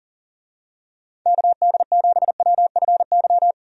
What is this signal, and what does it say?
Morse code, heard as a keyed tone, about 40 wpm, 700 Hz. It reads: KD7WFY
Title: USA Call Signs - 6 Characters - Repeated 2x - 40wpm